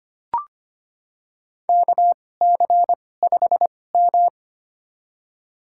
Morse code, heard as a keyed tone, about 25 wpm, spaced as for 18 wpm, 700 Hz. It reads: KC5M
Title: USA Call Signs - 4 Characters - 2x Character Spacing - 25wpm